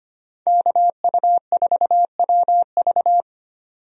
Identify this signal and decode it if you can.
Morse code, heard as a keyed tone, about 25 wpm, 700 Hz. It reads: KU4WV
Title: USA Call Signs - 5 Characters - 25wpm